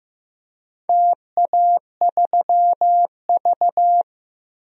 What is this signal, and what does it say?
Morse code, heard as a keyed tone, about 15 wpm, 700 Hz. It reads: TA3V
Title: DXCC-100 Call Signs - 4 Characters - 15wpm